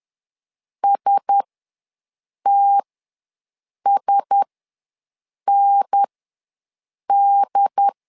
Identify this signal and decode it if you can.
Morse code, heard as a keyed tone, about 10 wpm, spaced as for 6 wpm, 800 Hz. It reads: STSND
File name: LEZIONI / 4 Q1S ws 10wpm 2208